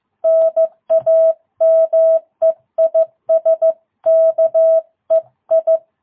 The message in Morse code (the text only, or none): NAME IS KEI